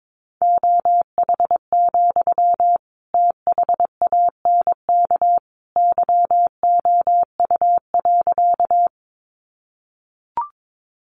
Morse code, heard as a keyed tone, about 22 wpm, 700 Hz. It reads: OH, THANK YOU.
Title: Sentences from Top 600 Words 22wpm